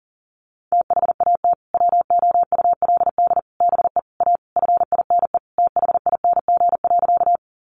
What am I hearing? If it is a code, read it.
Morse code, heard as a keyed tone, about 40 wpm, 700 Hz. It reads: THAT WOULD BE A FINE THING.
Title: Sentences from Top 500 Words 40wpm